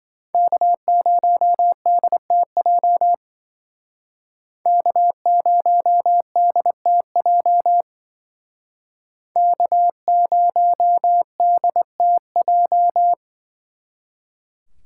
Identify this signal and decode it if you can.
Morse code, heard as a keyed tone, about 25 wpm, 700 Hz. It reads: K0DTJ K0DTJ K0DTJ
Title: Speed Racing - USA 1x3 Call Signs 20wpm